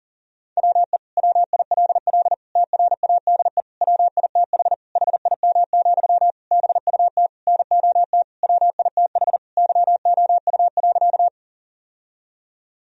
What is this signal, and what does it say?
Morse code, heard as a keyed tone, about 40 wpm, 700 Hz. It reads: WE WILL TRADE WITH HIM, BUT NOT WITH YOU.
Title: Sentences from Top 1000 Words - Review Part 4 - 40wpm